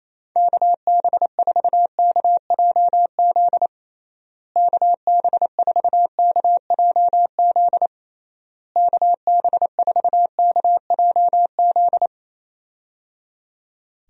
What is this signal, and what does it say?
Morse code, heard as a keyed tone, about 28 wpm, 700 Hz. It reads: KB4KJZ KB4KJZ KB4KJZ
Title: USA Call Signs - 6 Characters - Repeated 3x - 28wpm